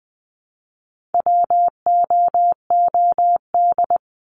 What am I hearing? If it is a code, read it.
Morse code, heard as a keyed tone, about 20 wpm, 700 Hz. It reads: WOOD
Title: Top 300 Words 20wpm